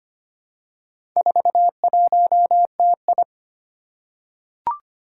Morse code, heard as a keyed tone, about 25 wpm, 700 Hz. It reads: 41TI
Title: Four Challenging Letter-Number Combinations 25wpm